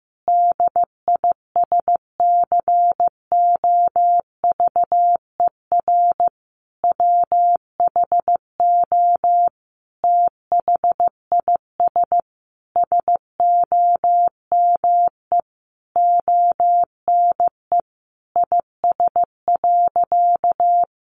Morse code, heard as a keyed tone, about 14 wpm, 700 Hz. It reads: DISCOVER WHO THIS SOME ONE IS.
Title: Sentences from Top 1100 Words 15wpm